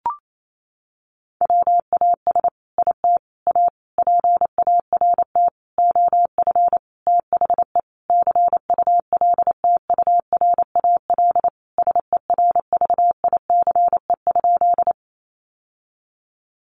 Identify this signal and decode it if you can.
Morse code, heard as a keyed tone, about 28 wpm, 700 Hz. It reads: WAS IT A PART OF THE CULTURAL SERVICE?
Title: Sentences from Top 1500 Words 28wpm